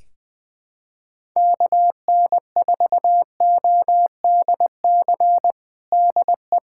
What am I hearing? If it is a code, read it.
Morse code, heard as a keyed tone, about 21 wpm, 700 Hz. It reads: KN4ODC DE
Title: USA Callsign DE Callsign 20wpm